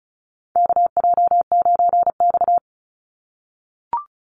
Morse code, heard as a keyed tone, about 34 wpm, 700 Hz. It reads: KJ9X
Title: NA SOTA Chaser Callsigns 35wpm